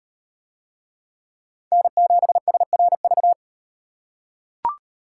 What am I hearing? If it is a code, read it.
Morse code, heard as a keyed tone, about 38 wpm, 700 Hz. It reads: N7SRV